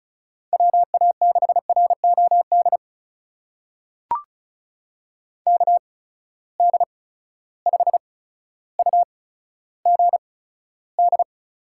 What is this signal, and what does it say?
Morse code, heard as a keyed tone, about 35 wpm, spaced as for 15 wpm, 700 Hz. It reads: WA6ROD KD5UGD
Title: USA Call Signs - 6 Characters - 8x Character Spacing - 35wpm